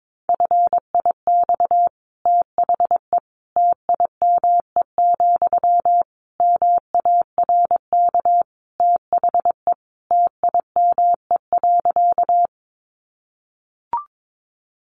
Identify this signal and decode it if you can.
Morse code, heard as a keyed tone, about 22 wpm, 700 Hz. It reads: FIX THE TIME, MARK THE TIME.
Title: Sentences from Top 1400 Words 22wpm